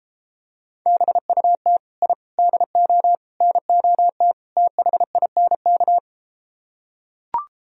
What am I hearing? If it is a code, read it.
Morse code, heard as a keyed tone, about 33 wpm, 700 Hz. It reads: BUT I DO NOT THINK